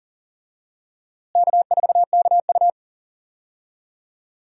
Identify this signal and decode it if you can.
Morse code, heard as a keyed tone, about 39 wpm, 700 Hz. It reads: K4KU
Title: USA Call Signs - 4 Characters - Repeated 3x - 40wpm